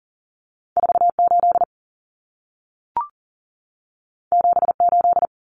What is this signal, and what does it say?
Morse code, heard as a keyed tone, about 40 wpm, 700 Hz. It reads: 48 78
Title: Common Cut Numbers - 2-Digit 40wpm